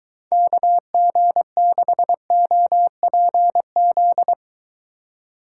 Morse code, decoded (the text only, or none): KG6OPZ